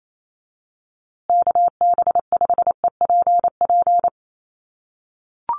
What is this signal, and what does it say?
Morse code, heard as a keyed tone, about 28 wpm, 700 Hz. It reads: KB5EPP
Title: USA Call Signs Repeated 2x 28wpm